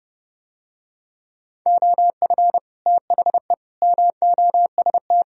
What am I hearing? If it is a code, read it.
Morse code, heard as a keyed tone, about 30 wpm, 700 Hz. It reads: OF THE MOST